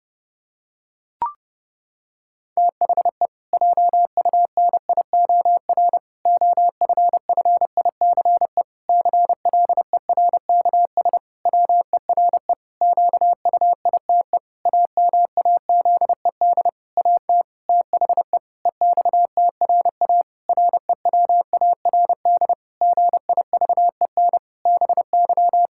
Morse code, heard as a keyed tone, about 30 wpm, 700 Hz. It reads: THE JUNIOR OFFICE CLERKS WERE QUITE AMAZED AT THE EXTRA REWARD GIVEN BY
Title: Pangram Challenge v2 - 30wpm